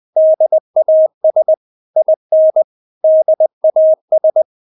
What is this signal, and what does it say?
Morse code, heard as a keyed tone, about 20 wpm, 630 Hz. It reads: DAS IN DAS
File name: Witz1 20wpm